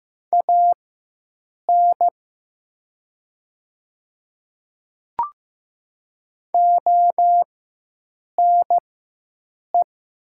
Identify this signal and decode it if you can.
Morse code, heard as a keyed tone, about 15 wpm, spaced as for 7 wpm, 700 Hz. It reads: AN ONE
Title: 4x Character Spacing - Top 100 Words - 15wpm